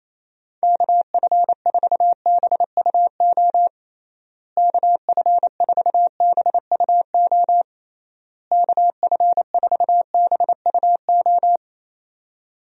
Morse code, heard as a keyed tone, about 28 wpm, 700 Hz. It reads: KF4BUO KF4BUO KF4BUO
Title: USA Call Signs - 6 Characters - Repeated 3x - 28wpm